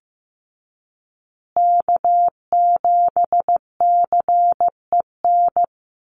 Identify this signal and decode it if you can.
Morse code, heard as a keyed tone, about 15 wpm, 700 Hz. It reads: K7CEN
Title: USA Call Signs Repeated 3x 15wpm